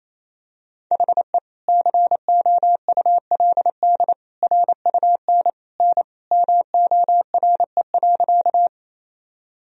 Morse code, heard as a keyed tone, about 28 wpm, 700 Hz. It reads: HE COULD RUN N MORE.
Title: Sentences from Top 300 Words and Abbreviations 28wpm